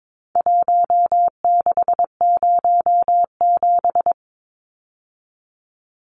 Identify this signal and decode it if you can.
Morse code, heard as a keyed tone, about 22 wpm, 700 Hz. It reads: 1607
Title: Numbers - Contest Style - 4-Digit 22wpm